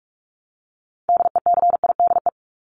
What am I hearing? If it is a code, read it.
Morse code, heard as a keyed tone, about 44 wpm, 700 Hz. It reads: DECIDE
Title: ED Suffix - Top 100 Words with Base Words 45wpm